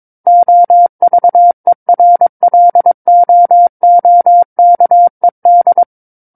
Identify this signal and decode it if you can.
Morse code, heard as a keyed tone, about 22 wpm, 700 Hz. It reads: OVERLOOKED
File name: Book One / 13-Chapter 13 - 22 WPM